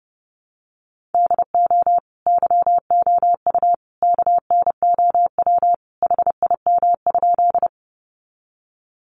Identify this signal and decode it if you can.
Morse code, heard as a keyed tone, about 30 wpm, 700 Hz. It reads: DO YOU KNOW HIM?
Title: Sentences from Top 100 Words 30wpm